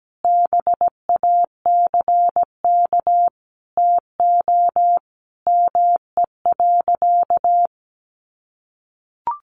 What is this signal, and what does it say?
Morse code, heard as a keyed tone, about 17 wpm, 700 Hz. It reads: BACK TO ME.
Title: Sentences from Top 600 Words 17wpm